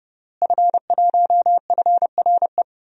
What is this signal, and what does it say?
Morse code, heard as a keyed tone, about 30 wpm, 700 Hz. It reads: F1FRE
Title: DXCC-100 Call Signs - 5 Characters - Repeated 3x - 30wpm